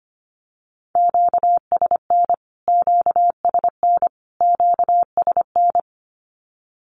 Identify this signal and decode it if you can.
Morse code, heard as a keyed tone, about 25 wpm, 700 Hz. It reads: QSN QSN QSN
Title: Q-Codes Repeated 3x 25wpm